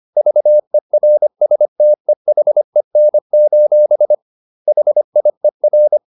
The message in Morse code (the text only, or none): VERSTEHEN: HIER